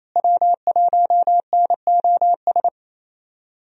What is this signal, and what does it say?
Morse code, heard as a keyed tone, about 27 wpm, 700 Hz. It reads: W1NOS
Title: USA Call Signs - 5 Characters - Repeated 2x - 28wpm